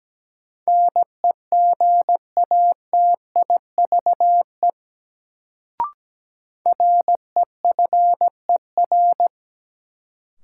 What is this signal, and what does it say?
Morse code, heard as a keyed tone, about 17 wpm, 700 Hz. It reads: NEGATIVE REFER
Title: Sentences from Top 1000 Words - New Word Drill - 17wpm